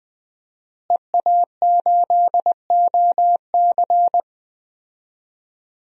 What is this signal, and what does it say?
Morse code, heard as a keyed tone, about 20 wpm, 700 Hz. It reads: EA8OC
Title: DXCC-100 Call Signs - 5 Characters - Repeated 3x - 20wpm